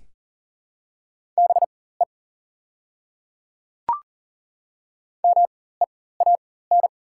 Morse code, heard as a keyed tone, about 37 wpm, spaced as for 18 wpm, 700 Hz. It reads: BE MEAN
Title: 4x Character Spacing - Top 200 Words - 40wpm